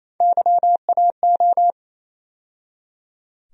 Morse code, heard as a keyed tone, about 27 wpm, 700 Hz. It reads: YAO
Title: USA Call Sign Postfixes - 3 Letters 28wpm